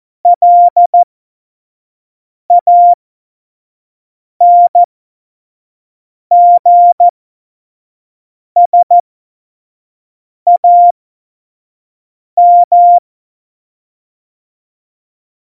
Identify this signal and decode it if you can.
Morse code, heard as a keyed tone, about 14 wpm, spaced as for 5 wpm, 700 Hz. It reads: LANGSAM
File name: Morsekurs / Witze / Witz 04. Dezember 2018, 25 F70